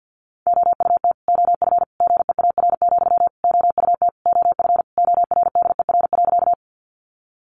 Morse code, heard as a keyed tone, about 50 wpm, 700 Hz. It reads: OUT OF GEAR, OUT OF ORDER.
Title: Sentences from Top 1400 Words 50wpm